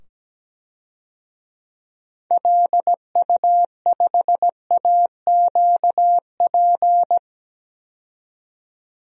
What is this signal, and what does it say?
Morse code, heard as a keyed tone, about 17 wpm, 700 Hz. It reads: LU5AQP
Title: DXCC-100 Call Signs - 6 Characters - 17wpm